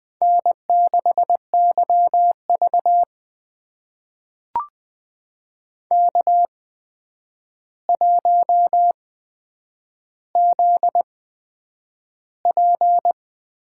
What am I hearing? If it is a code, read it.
Morse code, heard as a keyed tone, about 20 wpm, spaced as for 9 wpm, 700 Hz. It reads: N6YV K1ZP
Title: USA Call Signs - 4 Characters - 8x Character Spacing - 20wpm